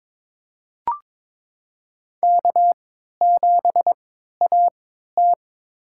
Morse code, heard as a keyed tone, about 23 wpm, spaced as for 13 wpm, 700 Hz. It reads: K7AT
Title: USA Call Signs - 4 Characters - 3x Character Spacing - 22wpm